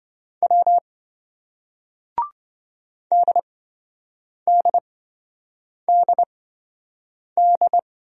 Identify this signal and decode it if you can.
Morse code, heard as a keyed tone, about 28 wpm, spaced as for 7 wpm, 700 Hz. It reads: W DDDD